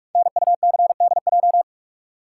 Morse code, heard as a keyed tone, about 44 wpm, 700 Hz. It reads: NUCDJ
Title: Code Groups - 5 Letters - 45wpm